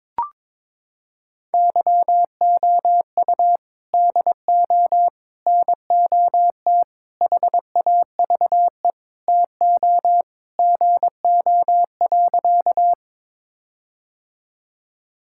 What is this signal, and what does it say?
Morse code, heard as a keyed tone, about 22 wpm, 700 Hz. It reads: YOU DO NOT HAVE TO GO.
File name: practice / 22wpm / Sentences with 4-Word Phrases - 22wpm